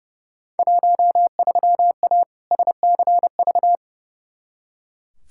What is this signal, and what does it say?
Morse code, heard as a keyed tone, about 32 wpm, 700 Hz. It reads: 13A SCV
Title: Field Day Exchanges Encore - 30wpm